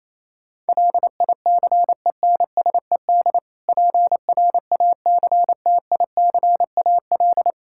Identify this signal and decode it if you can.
Morse code, heard as a keyed tone, about 28 wpm, 700 Hz. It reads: LICENSED PRACTICAL
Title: Practice QSO - Exchange 3 - More Info - 28wpm